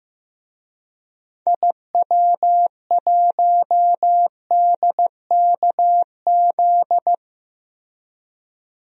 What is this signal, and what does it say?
Morse code, heard as a keyed tone, about 15 wpm, 700 Hz. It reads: IW1DKZ
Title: DXCC-100 Call Signs - 6 Characters - Repeated 3x - 15wpm